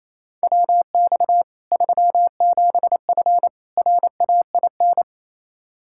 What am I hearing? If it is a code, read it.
Morse code, heard as a keyed tone, about 28 wpm, 700 Hz. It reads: WX 37F RAIN
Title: Practice QSO - Exchange 2 - Info - 28wpm